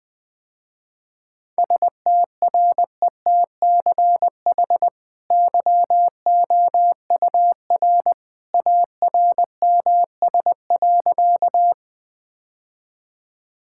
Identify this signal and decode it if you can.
Morse code, heard as a keyed tone, about 20 wpm, 700 Hz. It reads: STRETCH YOUR ARMS.
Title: Easy Sentences with Hints - 20wpm